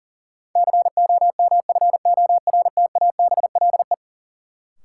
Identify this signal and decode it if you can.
Morse code, heard as a keyed tone, about 40 wpm, 700 Hz. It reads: COMFORTABLE
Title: Sentences from Top 1000 Words - Review Part 7 - New Word Drill - 40wpm